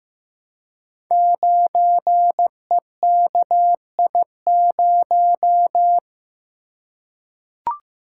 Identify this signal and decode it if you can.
Morse code, heard as a keyed tone, about 15 wpm, 700 Hz. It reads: 9EKI0
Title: Code Groups - 5 Letter:Numbers - 15wpm